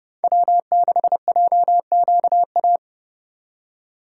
Morse code, heard as a keyed tone, about 31 wpm, 700 Hz. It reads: W6JQA
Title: USA Call Signs - 5 Characters - 3x Character Spacing - 30wpm